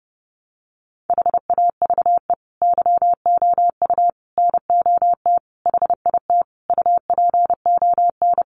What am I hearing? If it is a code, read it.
Morse code, heard as a keyed tone, about 28 wpm, 700 Hz. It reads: HAVE YOU NOT HIT UPON
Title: Sentences from Top 1200 Words 30wpm